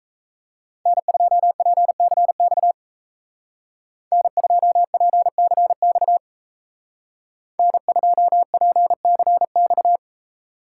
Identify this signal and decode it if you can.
Morse code, heard as a keyed tone, about 34 wpm, 700 Hz. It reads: N2PCX N2PCX N2PCX
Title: Speed Racing - USA 1x3 Call Signs 28wpm